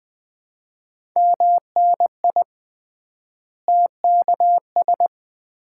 Morse code, heard as a keyed tone, about 20 wpm, 700 Hz. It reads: MNI TKS
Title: Common QSO Phrases - 3x Word-Spacing - 20wpm